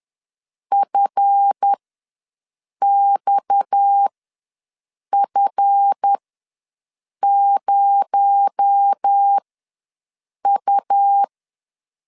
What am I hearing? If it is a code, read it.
Morse code, heard as a keyed tone, about 10 wpm, spaced as for 6 wpm, 800 Hz. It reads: FXF0U
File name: LEZIONI / 7 GF9Z / 7 GF ws 10wpm 3918